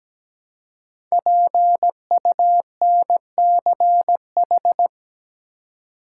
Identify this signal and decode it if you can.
Morse code, heard as a keyed tone, about 17 wpm, 700 Hz. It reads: PUNCH
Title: Five Letter Words - 17wpm